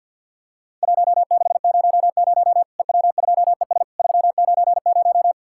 Morse code, heard as a keyed tone, about 50 wpm, 700 Hz. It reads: 1600 EW2ES 390
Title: Worked All Europe DX Contest - QTC Exchange - 50wpm